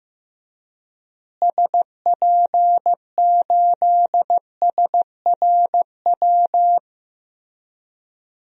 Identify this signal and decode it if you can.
Morse code, heard as a keyed tone, about 15 wpm, 700 Hz. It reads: SP8SRW
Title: DXCC-100 Call Signs - 6 Characters - Repeated 3x - 15wpm